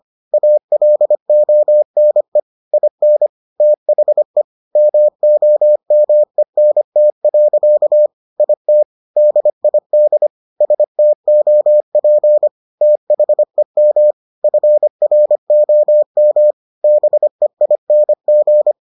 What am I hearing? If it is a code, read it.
Morse code, heard as a keyed tone, about 25 wpm, 600 Hz. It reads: ALONE IN THE MOMENT. IT DID STOP THEM FROM BEING